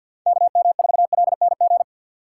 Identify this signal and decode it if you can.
Morse code, heard as a keyed tone, about 50 wpm, 700 Hz. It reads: KM4LNG
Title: USA Call Signs - 6 Characters - 6x Character Spacing - 50wpm